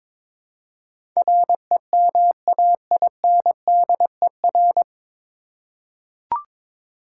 Morse code, heard as a keyed tone, about 21 wpm, 700 Hz. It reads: REMAINDER
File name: 9 Letter Words - First 3 Letters Given - 22wpm